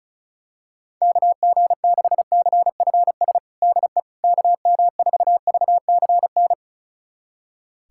Morse code, heard as a keyed tone, about 34 wpm, 700 Hz. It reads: KG6CFS DE KM4VCN
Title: USA Callsign DE Callsign - 6 Characters - 35wpm